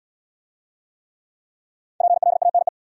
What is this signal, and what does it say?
Morse code, heard as a keyed tone, about 67 wpm, 700 Hz. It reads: PLANE